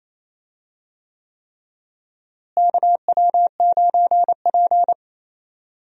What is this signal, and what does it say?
Morse code, heard as a keyed tone, about 28 wpm, 700 Hz. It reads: KW9P
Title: USA Call Signs - 4 Characters - Repeated 3x - 28wpm